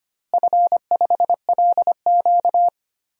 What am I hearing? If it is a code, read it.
Morse code, heard as a keyed tone, about 25 wpm, 700 Hz. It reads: F5LQ
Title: DXCC-100 Call Signs - 4 Characters - Repeated 3x - 25wpm